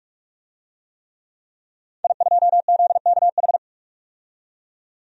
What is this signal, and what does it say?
Morse code, heard as a keyed tone, about 46 wpm, 700 Hz. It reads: I2ZKH
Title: DXCC-100 Call Signs - 5 Characters - 45wpm